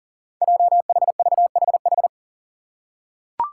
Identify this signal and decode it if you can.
Morse code, heard as a keyed tone, about 38 wpm, 700 Hz. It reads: JHVHH
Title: Five Challenging Letter-Number Combinations 40wpm